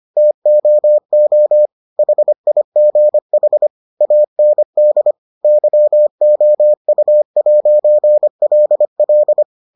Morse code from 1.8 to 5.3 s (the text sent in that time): HIGH AND